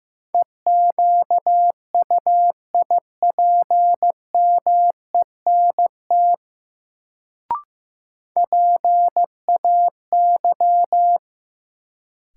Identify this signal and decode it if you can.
Morse code, heard as a keyed tone, about 15 wpm, 700 Hz. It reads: EQUIPMENT PAY
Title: Sentences from Top 700 Words - New Word Drill - 15wpm